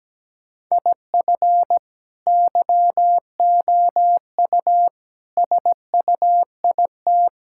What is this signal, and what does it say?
Morse code, heard as a keyed tone, about 17 wpm, 700 Hz. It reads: IF YOU SUIT
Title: Sentences from Top 1600 Words 17wpm